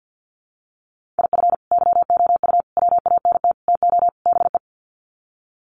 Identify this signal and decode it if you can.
Morse code, heard as a keyed tone, about 50 wpm, 700 Hz. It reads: IF YOU WANT TO BE
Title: Sets of 5 Words Encore 50wpm